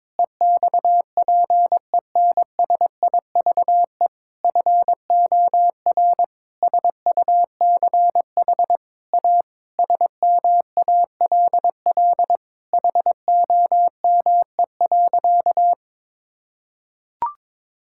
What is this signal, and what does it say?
Morse code, heard as a keyed tone, about 22 wpm, 700 Hz. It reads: EXPENSIVE FOR SUCH A SMALL HOME.
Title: Sentences from Top 1000 Words - Review Part 6 - 22wpm